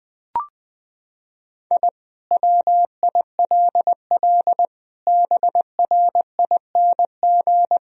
Morse code, heard as a keyed tone, about 21 wpm, 700 Hz. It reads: I WILL BRING
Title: Sentences from Top 1000 Words - Review Part 3 - 20wpm